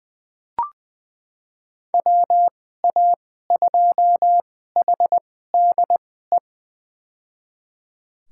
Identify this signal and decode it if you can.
Morse code, heard as a keyed tone, about 21 wpm, spaced as for 15 wpm, 700 Hz. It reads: WA2HDE
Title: USA Call Signs - 6 Characters - 2x Character Spacing - 20wpm